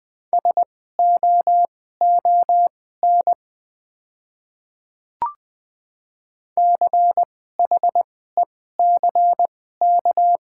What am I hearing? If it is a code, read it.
Morse code, heard as a keyed tone, about 20 wpm, spaced as for 14 wpm, 690 Hz. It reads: SOON CHECK
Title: 2x Character Spacing - Top 500 Words - 20wpm